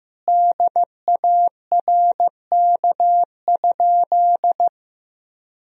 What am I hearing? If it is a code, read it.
Morse code, heard as a keyed tone, about 15 wpm, 700 Hz. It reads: DARK?
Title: Sentences from Top 1000 Words - Review Part 1 - 15wpm